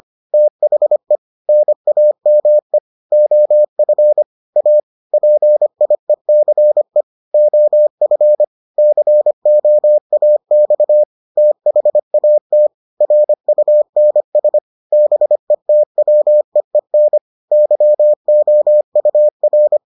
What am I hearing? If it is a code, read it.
Morse code, heard as a keyed tone, about 25 wpm, 600 Hz. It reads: THE NAME OF A PIECE OF COAX THAT RUNS BETWEEN YOUR